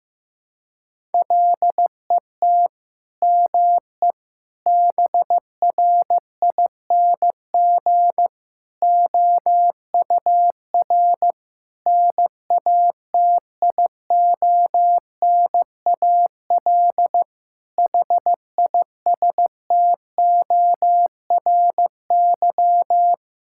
Morse code, heard as a keyed tone, about 16 wpm, 700 Hz. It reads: LET ME BRING OUR NATIONAL HISTORY
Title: Sentences from Top 1000 Words - Review Part 6 - 15wpm